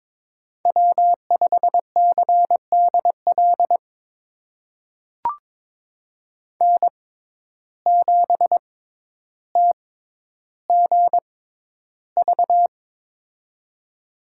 Morse code, heard as a keyed tone, about 21 wpm, spaced as for 11 wpm, 700 Hz. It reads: W5CDL N7TGV
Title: USA Call Signs - 5 Characters - 6x Character Spacing - 22wpm